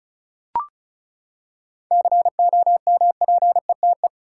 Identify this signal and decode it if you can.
Morse code, heard as a keyed tone, about 36 wpm, 700 Hz. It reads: COMPETE
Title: Seven Letter Words - 35wpm